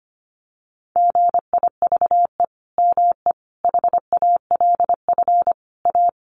Morse code, heard as a keyed tone, about 25 wpm, 700 Hz. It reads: GIVE ME HALF A
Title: Sentences from Top 1500 Words 25wpm